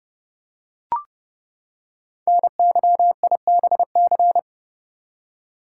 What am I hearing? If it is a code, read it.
Morse code, heard as a keyed tone, about 30 wpm, 700 Hz. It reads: NYIBC
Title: Code Groups - 5 Letter:Numbers - 30wpm